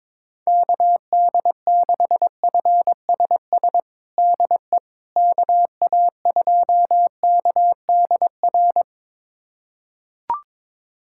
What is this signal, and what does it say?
Morse code, heard as a keyed tone, about 22 wpm, 700 Hz. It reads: KD6FSS DE KA2KDR
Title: USA Callsign DE Callsign 22wpm